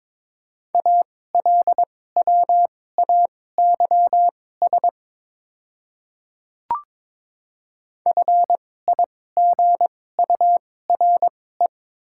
Morse code, heard as a keyed tone, about 22 wpm, spaced as for 16 wpm, 700 Hz. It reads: ALWAYS FIGURE
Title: 2x Character Spacing - Top 500 Words - 22wpm